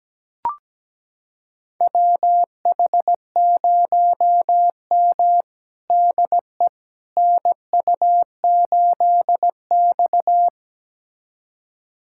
Morse code, heard as a keyed tone, about 17 wpm, 700 Hz. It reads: WH0M DE NU8X
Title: USA Callsign DE Callsign - 4 Characters - 17wpm